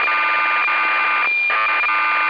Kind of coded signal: cassette data (Kansas City type)